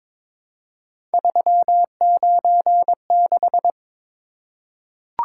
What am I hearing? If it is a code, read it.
Morse code, heard as a keyed tone, about 22 wpm, 700 Hz. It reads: 396
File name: practice / 22wpm / Three Letter-Number Combinations - 22wpm